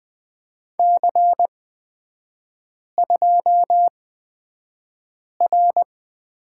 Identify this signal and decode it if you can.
Morse code, heard as a keyed tone, about 20 wpm, spaced as for 5 wpm, 700 Hz. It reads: C2R
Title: 224 Lesson 36e - BT - Three Characters 20wpm